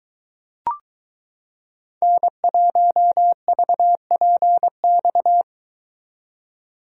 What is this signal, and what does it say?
Morse code, heard as a keyed tone, about 23 wpm, 700 Hz. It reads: N1VPX